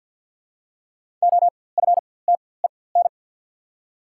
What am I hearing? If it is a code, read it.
Morse code, heard as a keyed tone, about 53 wpm, spaced as for 24 wpm, 690 Hz. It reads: OFTEN
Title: 4x Character Spacing - Top 300 Words - 50wpm